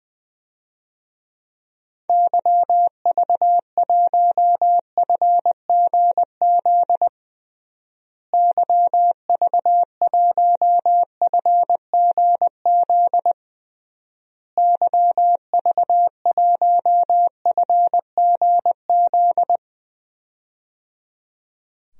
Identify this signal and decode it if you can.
Morse code, heard as a keyed tone, about 20 wpm, 700 Hz. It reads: YV1FGZ YV1FGZ YV1FGZ
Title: DXCC-100 Call Signs - 6 Characters - Repeated 3x - 20wpm